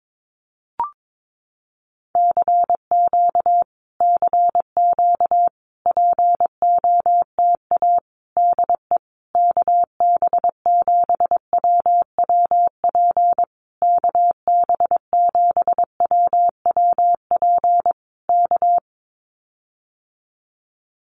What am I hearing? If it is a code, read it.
Morse code, heard as a keyed tone, about 22 wpm, 700 Hz. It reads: CQ CQ POTA DE KB7WWP KB7WWP K
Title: Common QSO Phrases Encore 22wpm